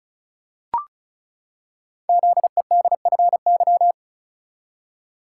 Morse code, heard as a keyed tone, about 36 wpm, 690 Hz. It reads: ZEDFY
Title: Code Groups - 5 Letters - 35wpm